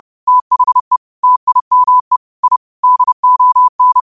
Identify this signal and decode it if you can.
Morse code, heard as a keyed tone, about 30 wpm, 1000 Hz. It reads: THE TIME I DON